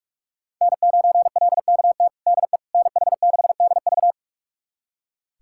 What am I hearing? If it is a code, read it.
Morse code, heard as a keyed tone, about 41 wpm, 700 Hz. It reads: N9LKT DE NH6DV
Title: USA Callsign DE Callsign - 5 Characters - 45wpm